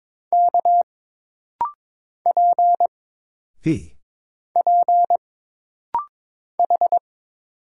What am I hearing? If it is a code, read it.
Morse code, heard as a keyed tone, about 22 wpm, 700 Hz. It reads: K P P H
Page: This rapid-fire timing is geared to push your ICR (Instant Character Recognition) skills.